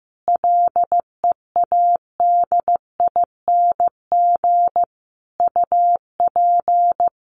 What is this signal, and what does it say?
Morse code, heard as a keyed tone, about 15 wpm, 700 Hz. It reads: LEADING UP
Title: Sentences from Top 1100 Words 15wpm